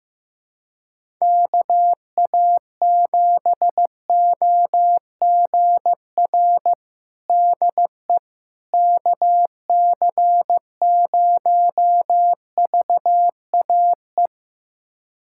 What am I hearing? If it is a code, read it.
Morse code, heard as a keyed tone, about 15 wpm, 700 Hz. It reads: KA7OGR DE KC0VAE
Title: USA Callsign DE Callsign - 6 Characters - 15wpm